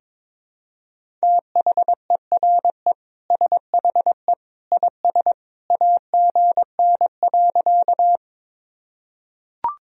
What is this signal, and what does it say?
Morse code, heard as a keyed tone, about 21 wpm, 700 Hz. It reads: THERE SHE IS AGN.
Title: Sentences from Top 200 Words and Abbreviations 22wpm